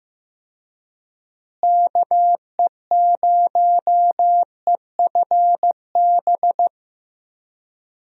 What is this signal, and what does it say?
Morse code, heard as a keyed tone, about 15 wpm, 700 Hz. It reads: KE0EFB
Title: USA Call Signs - 6 Characters - Repeated 3x - 15wpm